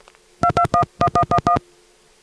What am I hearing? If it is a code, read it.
DTMF, heard as telephone touch tones, tones of about 80 ms. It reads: 2211111